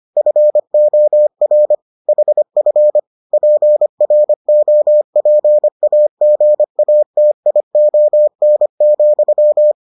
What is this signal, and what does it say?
Morse code, heard as a keyed tone, about 25 wpm, 600 Hz. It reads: FOR HF PROPAGATION,